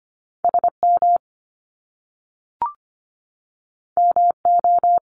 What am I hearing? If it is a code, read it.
Morse code, heard as a keyed tone, about 25 wpm, 700 Hz. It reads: SM MO
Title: Top Two Letter Combinations in Words 25wpm